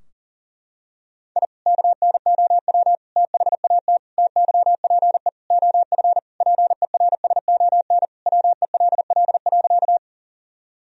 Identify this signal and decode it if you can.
Morse code, heard as a keyed tone, about 40 wpm, 700 Hz. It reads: I KNOW THAT TYPE OF PERSON WELL.
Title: Sentences from Top 1000 Words - Review Part 3 - 40wpm